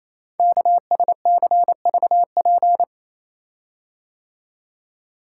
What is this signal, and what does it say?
Morse code, heard as a keyed tone, about 29 wpm, 700 Hz. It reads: KSCVP
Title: Code Groups - 5 Letters - 28wpm